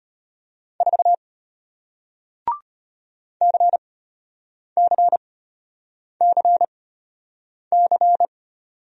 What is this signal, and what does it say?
Morse code, heard as a keyed tone, about 33 wpm, spaced as for 8 wpm, 700 Hz. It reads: 4 CCCC